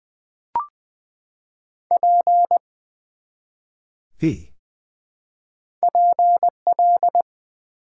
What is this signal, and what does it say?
Morse code, heard as a keyed tone, about 20 wpm, 700 Hz. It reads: P PL
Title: Build A Word - 1 Letter At A Time - 3 to 5-Letter Words - 20wpm